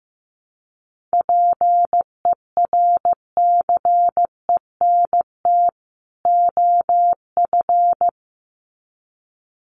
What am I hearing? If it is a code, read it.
Morse code, heard as a keyed tone, about 15 wpm, 700 Hz. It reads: PERCENT OF
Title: Sets of 2 Words 15wpm